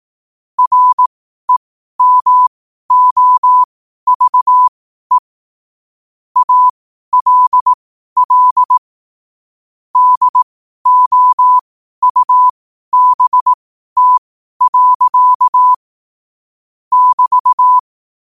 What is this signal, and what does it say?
Morse code, heard as a keyed tone, about 18 wpm, spaced as for 12 wpm, 1000 Hz. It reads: REMOVE ALL DOUBT. <BT>
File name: qotdfiles / 12wpm